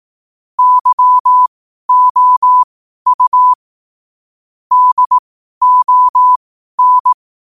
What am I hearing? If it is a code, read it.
Morse code, heard as a keyed tone, about 18 wpm, spaced as for 12 wpm, 1000 Hz. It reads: YOU DON
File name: qotdfiles / 12wpm